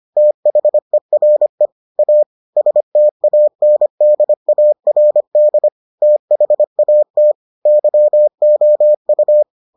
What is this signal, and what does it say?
Morse code, heard as a keyed tone, about 25 wpm, 600 Hz. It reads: THERE A STANDARD THAT YOU